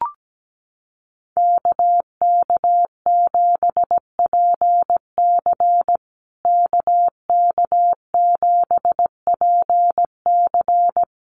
Morse code, heard as a keyed tone, about 17 wpm, 700 Hz. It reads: KK7PC KK7PC
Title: USA Call Signs Repeated 3x 17wpm